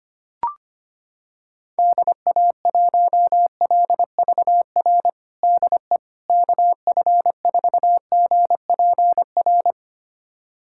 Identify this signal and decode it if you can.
Morse code, heard as a keyed tone, about 25 wpm, 700 Hz. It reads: DA1LVR DE KF4GPR
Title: USA Callsign DE Callsign - 6 Characters - 25wpm